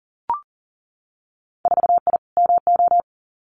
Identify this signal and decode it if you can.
Morse code, heard as a keyed tone, about 41 wpm, 700 Hz. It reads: 4I MO
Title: Winter Field Day Exchanges 40wpm